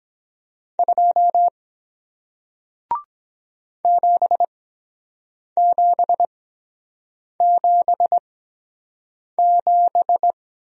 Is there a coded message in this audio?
Yes: Morse code from 0.8 to 10.3 s, heard as a keyed tone, about 23 wpm, spaced as for 7 wpm, 700 Hz. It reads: 2 7777